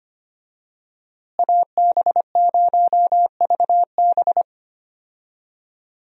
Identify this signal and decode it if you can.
Morse code, heard as a keyed tone, about 25 wpm, 700 Hz. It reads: AB0VB
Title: USA Call Signs - 5 Characters - Repeated 2x - 25wpm